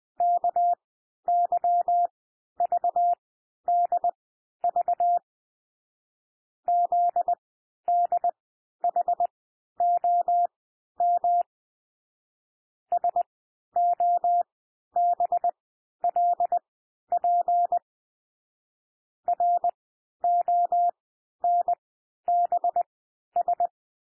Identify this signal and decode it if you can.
Morse code, heard as a keyed tone, about 20 wpm, spaced as for 11 wpm, 700 Hz. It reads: KYVDV ZDHOM SOBLP RONBS
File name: cwmp3 / L1 SOUND LETTERS 20 wpm